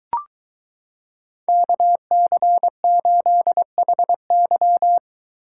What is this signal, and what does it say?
Morse code, heard as a keyed tone, about 23 wpm, 700 Hz. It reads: KC8HY